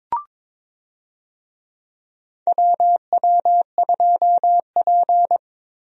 Morse code, heard as a keyed tone, about 22 wpm, 700 Hz. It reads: WW2P
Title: USA Call Signs - 4 Characters - Repeated 2x - 22wpm